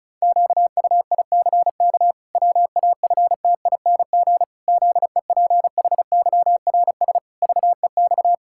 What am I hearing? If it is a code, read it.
Morse code, heard as a keyed tone, about 36 wpm, 700 Hz. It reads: QUICK WAFTING ZEPHYRS VEX
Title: Pangram Challenge v2 - 35wpm